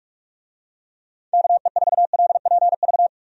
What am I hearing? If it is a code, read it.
Morse code, heard as a keyed tone, about 43 wpm, 700 Hz. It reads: KE4LPV
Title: USA Call Signs - 6 Characters - 45wpm